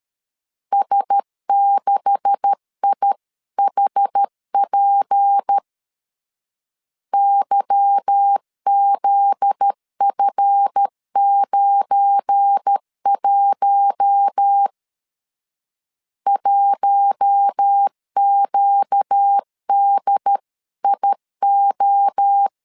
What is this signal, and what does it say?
Morse code, heard as a keyed tone, about 13 wpm, 800 Hz. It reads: S6IHP YZF91 1QDIO
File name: CW / DET / 1 9 1500C S A 12wpm 3800